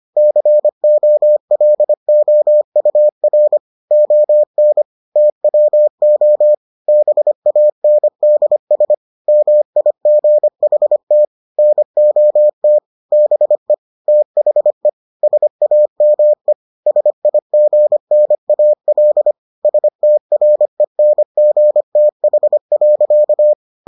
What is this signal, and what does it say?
Morse code, heard as a keyed tone, about 25 wpm, 600 Hz. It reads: COLOUR ON TWO BANDS MIGHT NOT BE THE SAME SIGNAL STRENGTH.